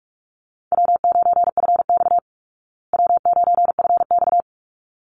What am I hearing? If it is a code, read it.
Morse code, heard as a keyed tone, about 44 wpm, 700 Hz. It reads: W9FX W9FX
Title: Easy Call Signs Repeated 2x - 45wpm